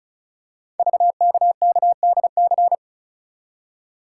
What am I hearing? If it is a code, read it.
Morse code, heard as a keyed tone, about 36 wpm, 690 Hz. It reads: VKKDC
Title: Code Groups - 5 Letters - 35wpm